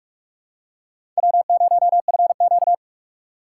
Morse code, heard as a keyed tone, about 46 wpm, 700 Hz. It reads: W0FQ
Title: USA Call Signs - 4x Character Spacing - 45wpm